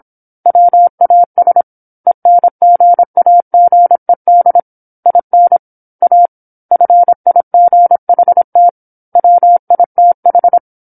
Morse code, heard as a keyed tone, about 27 wpm, 700 Hz. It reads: WAS ENGAGED IN A FIGHT WITH
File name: 08-Chapter-08 26 WPM